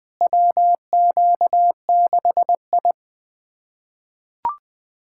Morse code, heard as a keyed tone, about 20 wpm, 700 Hz. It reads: WQ6I
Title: USA Callsign DE Callsign - 4 Characters - 20wpm